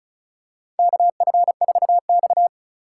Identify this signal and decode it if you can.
Morse code, heard as a keyed tone, about 36 wpm, 700 Hz. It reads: KF4X
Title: USA Call Signs - 4 Characters - 2x Character Spacing - 35wpm